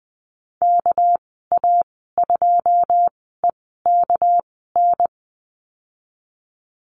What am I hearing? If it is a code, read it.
Morse code, heard as a keyed tone, about 20 wpm, spaced as for 14 wpm, 700 Hz. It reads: KA2EKN
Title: 207 Lesson 33g - Letter j - Call Signs 20wpm